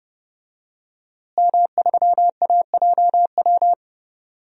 Morse code, heard as a keyed tone, about 31 wpm, 700 Hz. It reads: M3AJW
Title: DXCC-100 Call Signs - 5 Characters - Repeated 2x - 30wpm